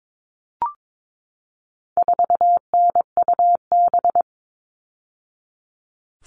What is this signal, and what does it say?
Morse code, heard as a keyed tone, about 22 wpm, 700 Hz. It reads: 4NUB
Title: All Cut Numbers - 4-Digit 22wpm